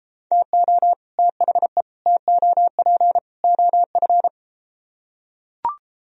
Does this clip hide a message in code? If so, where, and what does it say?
Yes: Morse code from 0.3 to 4.3 s, heard as a keyed tone, about 34 wpm, 700 Hz. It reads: TO THE TOP OF